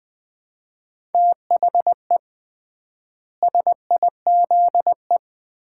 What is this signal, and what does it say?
Morse code, heard as a keyed tone, about 20 wpm, 700 Hz. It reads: THE SIZE
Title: 3x Word-Spacing - Sets of 4 Words - 20wpm